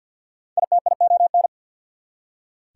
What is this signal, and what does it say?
Morse code, heard as a keyed tone, about 67 wpm, spaced as for 58 wpm, 700 Hz. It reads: ITION
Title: Speed-Racing Prefixes and Suffixes 50wpm